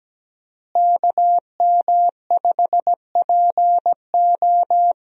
Morse code, heard as a keyed tone, about 17 wpm, 700 Hz. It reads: KM5PO
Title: USA Call Signs - 5 Characters - Repeated 2x - 17wpm